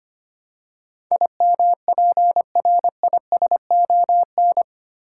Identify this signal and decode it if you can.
Morse code, heard as a keyed tone, about 25 wpm, 700 Hz. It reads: IMPRISON
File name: 8 Letter Words - First 3 Letters Given - 25wpm